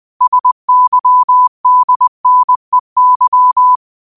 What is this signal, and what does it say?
Morse code, heard as a keyed tone, about 20 wpm, 1000 Hz. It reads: SYDNEY
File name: qotdfiles / 20wpm